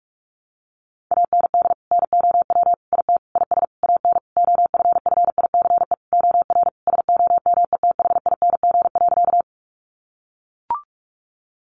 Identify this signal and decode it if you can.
Morse code, heard as a keyed tone, about 44 wpm, 700 Hz. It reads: AND NOW IT IS AN OFFICE OR SOMETHING.
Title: Sentences from Top 1000 Words 45wpm